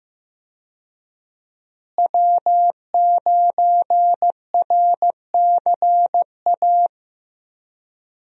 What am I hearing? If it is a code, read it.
Morse code, heard as a keyed tone, about 15 wpm, 700 Hz. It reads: W9RCA
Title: USA Call Signs - 5 Characters - Repeated 2x - 15wpm